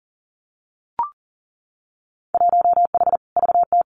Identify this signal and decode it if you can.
Morse code, heard as a keyed tone, about 40 wpm, 700 Hz. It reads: 1H VT
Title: Winter Field Day Exchanges 40wpm